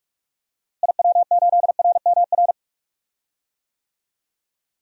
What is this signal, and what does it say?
Morse code, heard as a keyed tone, about 44 wpm, 700 Hz. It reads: IW8RMR
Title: DXCC-100 Call Signs - 6 Characters - Repeated 3x - 45wpm